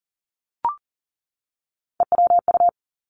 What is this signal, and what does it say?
Morse code, heard as a keyed tone, about 40 wpm, 700 Hz. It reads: EWU
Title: Three Challenging Letter-Number Combinations 40wpm